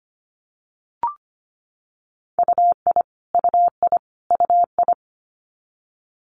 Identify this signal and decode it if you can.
Morse code, heard as a keyed tone, about 25 wpm, 700 Hz. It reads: UI UI UI
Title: Multi-letter Phonemes Repeated 3x 25wpm